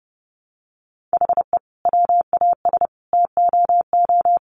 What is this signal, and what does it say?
Morse code, heard as a keyed tone, about 30 wpm, 700 Hz. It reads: HE WAS TOO
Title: Sentences from Top 600 Words 30wpm